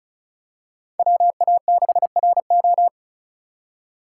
Morse code, heard as a keyed tone, about 36 wpm, 700 Hz. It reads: WA6RO
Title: USA Call Signs - 8x Character Spacing - 35wpm